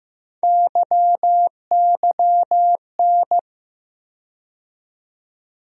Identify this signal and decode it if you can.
Morse code, heard as a keyed tone, about 15 wpm, 700 Hz. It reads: YYN
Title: USA Call Sign Postfixes - 3 Letters 15wpm